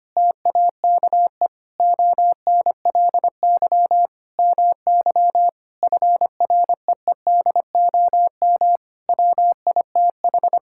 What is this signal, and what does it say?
Morse code, heard as a keyed tone, about 25 wpm, 700 Hz. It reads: TAKE ONLY MY FREEDOM WITH
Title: Sentences from Top 1000 Words - Review Part 5 - 25wpm